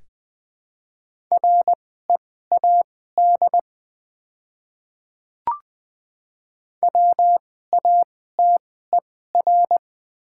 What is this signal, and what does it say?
Morse code, heard as a keyed tone, about 20 wpm, spaced as for 15 wpm, 700 Hz. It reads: READ WATER
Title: 2x Character Spacing - Top 300 Words - 20wpm